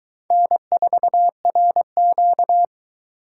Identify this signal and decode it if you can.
Morse code, heard as a keyed tone, about 22 wpm, 700 Hz. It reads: N4RQ